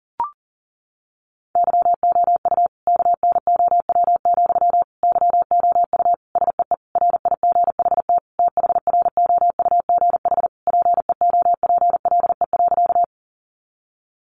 Morse code, heard as a keyed tone, about 40 wpm, 700 Hz. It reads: YOU KNOW, YOU SEE RIGHT THROUGH PEOPLE.
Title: Sentences from Top 200 Words 40wpm